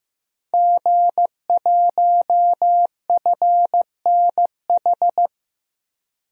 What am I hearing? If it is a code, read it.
Morse code, heard as a keyed tone, about 15 wpm, 700 Hz. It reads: G1FNH
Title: DXCC-100 Call Signs - 5 Characters - 15wpm